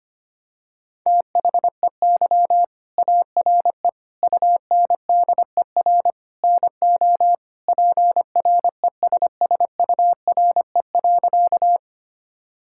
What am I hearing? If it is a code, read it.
Morse code, heard as a keyed tone, about 25 wpm, 700 Hz. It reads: THEY ARE UNDER NO PRESSURE.
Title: Sentences from Top 1000 Words - Review Part 6 - 25wpm